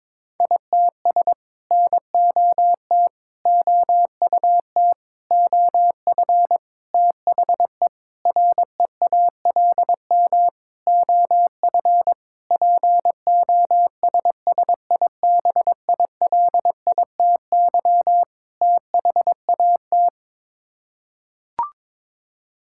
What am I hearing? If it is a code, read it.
Morse code, heard as a keyed tone, about 22 wpm, 700 Hz. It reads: ITS NOT OUT OF THE REALM OF POSSIBILITY THAT
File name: Sets of 9 Words - 22wpm